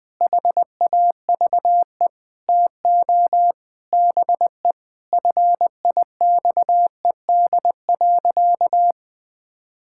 Morse code, heard as a keyed tone, about 20 wpm, 700 Hz. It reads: HAVE TO BE FIXED.
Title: Sentences with 5-Word Phrases - Spoken Beginning - 20wpm